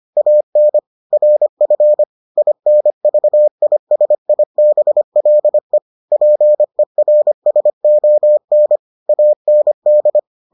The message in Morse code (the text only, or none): AN RF INVISIBLE PERSON AND